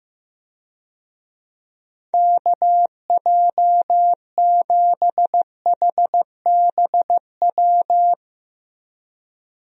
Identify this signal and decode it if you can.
Morse code, heard as a keyed tone, about 15 wpm, 700 Hz. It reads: KJ7HBW
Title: USA Call Signs - 6 Characters - Repeated 3x - 15wpm